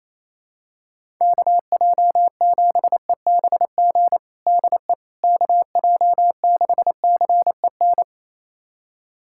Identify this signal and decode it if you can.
Morse code, heard as a keyed tone, about 28 wpm, 700 Hz. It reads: KJ7EBG DE KJ6CEN
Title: USA Callsign DE Callsign - 6 Characters - 28wpm